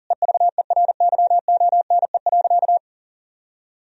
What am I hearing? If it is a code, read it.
Morse code, heard as a keyed tone, about 40 wpm, 700 Hz. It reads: EVERYONE.
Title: Sentences from Top 1000 Words - Review Part 3 - 40wpm